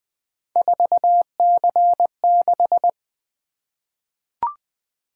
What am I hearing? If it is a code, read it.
Morse code, heard as a keyed tone, about 20 wpm, 700 Hz. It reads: 4C6
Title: Three Letter-Number Combinations - 20wpm